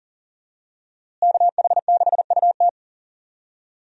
Morse code, heard as a keyed tone, about 40 wpm, 700 Hz. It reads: KH6UT
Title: USA Call Signs - 5 Characters - Repeated 2x - 40wpm